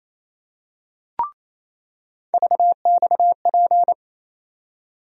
Morse code, heard as a keyed tone, about 27 wpm, 700 Hz. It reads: VXP
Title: USA Call Sign Postfixes - 3 Letters 28wpm